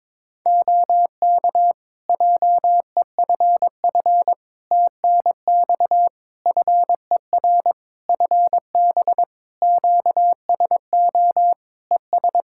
Morse code, heard as a keyed tone, about 22 wpm, 700 Hz. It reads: OK JEFF TNX FER FB QSO ES
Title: Practice QSO - Exchange 4 - Ending - 22wpm